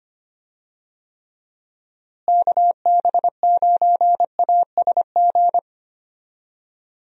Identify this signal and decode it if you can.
Morse code, heard as a keyed tone, about 25 wpm, 700 Hz. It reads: KB9ASG
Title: USA Call Signs - 6 Characters - Repeated 3x - 25wpm